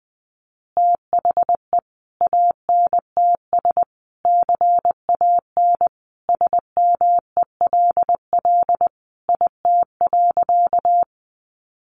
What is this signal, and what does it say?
Morse code, heard as a keyed tone, about 20 wpm, 700 Hz. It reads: THE ANTS CAN SMELL IT.
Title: Easy Phrases 20wpm